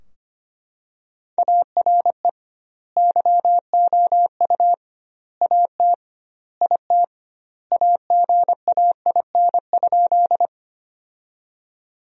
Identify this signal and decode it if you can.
Morse code, heard as a keyed tone, about 25 wpm, 700 Hz. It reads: ARE YOU AT IT AGAIN?
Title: 2x Word-Spacing - Sentences from Top 200 Words - 25wpm